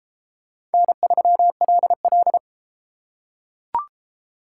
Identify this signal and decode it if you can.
Morse code, heard as a keyed tone, about 32 wpm, 700 Hz. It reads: N3LL